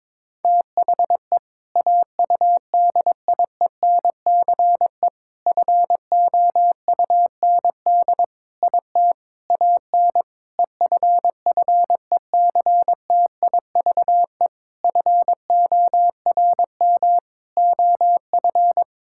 Morse code, heard as a keyed tone, about 22 wpm, 700 Hz. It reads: THE AUDIENCE FOUND IT AN EFFECTIVE FORM OF
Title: Sentences from Top 1000 Words - Review Part 6 - 22wpm